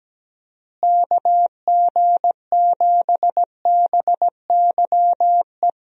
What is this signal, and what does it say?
Morse code, heard as a keyed tone, about 17 wpm, 700 Hz. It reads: KG7BYE
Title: USA Call Signs - 6 Characters - Repeated 2x - 17wpm